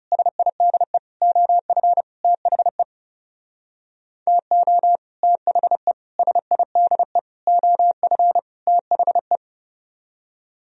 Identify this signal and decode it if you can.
Morse code, heard as a keyed tone, about 38 wpm, 700 Hz. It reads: SIDE OF THE TO THE SIDE OF THE